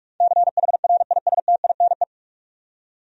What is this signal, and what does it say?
Morse code, heard as a keyed tone, about 46 wpm, 700 Hz. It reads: CHRISTINE
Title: USA Common Names - 45wpm